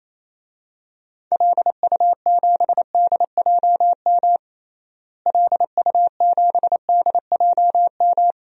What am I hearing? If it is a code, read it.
Morse code, heard as a keyed tone, about 28 wpm, 700 Hz. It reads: LU7DJM LU7DJM
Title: DXCC-100 Call Signs - 6 Characters - Repeated 2x - 28wpm